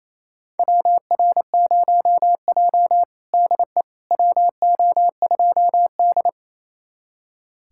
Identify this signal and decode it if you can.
Morse code, heard as a keyed tone, about 28 wpm, 700 Hz. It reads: WR0J DE WO2D
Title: USA Callsign DE Callsign - 4 Characters - 28wpm